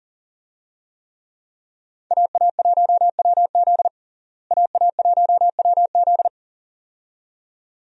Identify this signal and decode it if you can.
Morse code, heard as a keyed tone, about 37 wpm, 700 Hz. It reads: AA1WZ AA1WZ
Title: USA Call Signs - 5 Characters - Repeated 2x - 40wpm